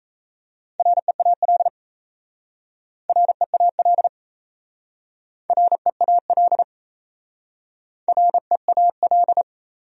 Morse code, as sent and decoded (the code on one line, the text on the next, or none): .-. .- .-.. / .-. .- .-.. / .-. .- .-.. / .-. .- .-..
REAL REAL REAL REAL